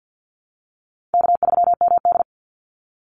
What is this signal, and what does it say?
Morse code, heard as a keyed tone, about 48 wpm, 700 Hz. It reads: K3MD
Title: Call Signs Repeated 3x - 50wpm